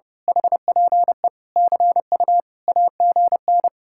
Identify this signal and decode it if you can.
Morse code, heard as a keyed tone, about 28 wpm, 700 Hz. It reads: HPE CU AGN
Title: Speed-Racing Common QSO Phrases - 30wpm